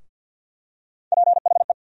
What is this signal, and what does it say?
Morse code, heard as a keyed tone, about 53 wpm, 700 Hz. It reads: PHE
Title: USA Call Sign Postfixes - 3 Letters 50wpm